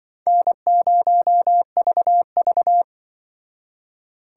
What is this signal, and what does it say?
Morse code, heard as a keyed tone, about 24 wpm, 700 Hz. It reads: N0VV
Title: Speed Racing - USA 1x2 Call Signs 20wpm